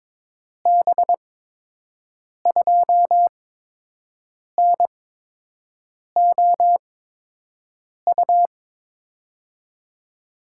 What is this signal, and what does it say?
Morse code, heard as a keyed tone, about 22 wpm, spaced as for 6 wpm, 700 Hz. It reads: B2NOU
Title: USA Call Signs - 8x Character Spacing - 22wpm